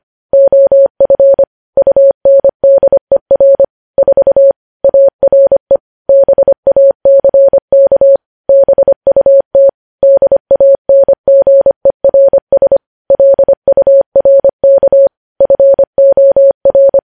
Morse code, heard as a keyed tone, about 25 wpm, 550 Hz. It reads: OF UNDER 4 ARE BACK BUT DANGERS LURK FOR
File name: podcast / News headlines-1x-25x25